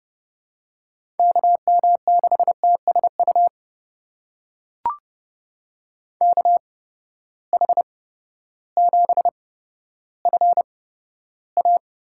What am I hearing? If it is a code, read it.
Morse code, heard as a keyed tone, about 32 wpm, spaced as for 14 wpm, 700 Hz. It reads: KM6TSU KH7FA
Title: USA Call Signs - 8x Character Spacing - 30wpm